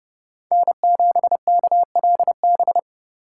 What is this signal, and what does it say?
Morse code, heard as a keyed tone, about 29 wpm, 700 Hz. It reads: N7KLB